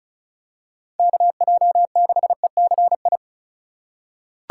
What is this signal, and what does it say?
Morse code, heard as a keyed tone, about 36 wpm, 700 Hz. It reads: KJ6ECI
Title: USA Call Signs - 6 Characters - 2x Character Spacing - 35wpm